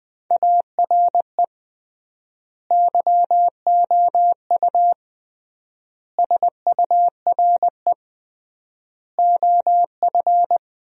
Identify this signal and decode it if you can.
Morse code, heard as a keyed tone, about 21 wpm, 700 Hz. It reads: ARE YOU SURE OF
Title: 3x Word-Spacing - Sentences from Top 300 Words - 20wpm